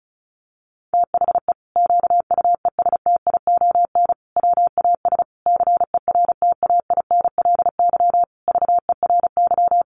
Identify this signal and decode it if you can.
Morse code, heard as a keyed tone, about 34 wpm, 700 Hz. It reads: THE QUESTION WAS CERTAINLY VERY
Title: Sentences from Top 1200 Words 35wpm